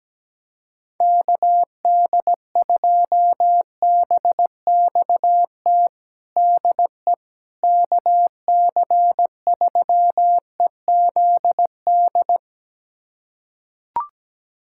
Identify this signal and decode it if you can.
Morse code, heard as a keyed tone, about 17 wpm, 700 Hz. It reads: KD2BXT DE KC3EZD
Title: USA Callsign DE Callsign 17wpm